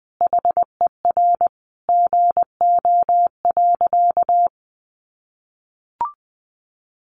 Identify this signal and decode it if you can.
Morse code, heard as a keyed tone, about 20 wpm, 700 Hz. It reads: HER GO.
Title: Sentences from Top 1500 Words 20wpm